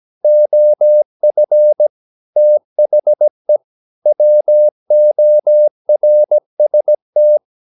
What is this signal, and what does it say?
Morse code, heard as a keyed tone, about 17 wpm, 600 Hz. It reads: OF THE WORST
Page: Romanen Never af Ken Follet på engelsk.